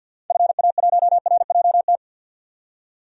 Morse code, heard as a keyed tone, about 48 wpm, 700 Hz. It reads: FA1RJT
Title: DXCC-100 Call Signs - 6 Characters - 50wpm